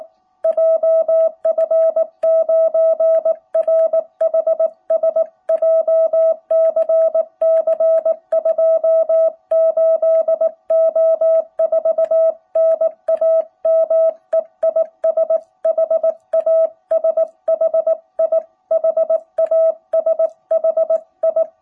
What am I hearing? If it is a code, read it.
Morse code, heard as a keyed tone, about 19 wpm, 660 Hz. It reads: JF9RHS JCC 28O4 NAME IS HASHI HASHI
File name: jh8cbh / 2021cw